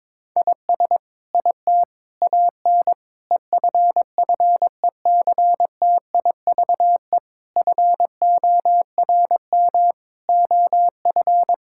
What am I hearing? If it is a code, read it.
Morse code, heard as a keyed tone, about 22 wpm, 700 Hz. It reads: IS IT AN EFFECTIVE FORM OF
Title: Sentences from Top 1000 Words - Review Part 6 - 22wpm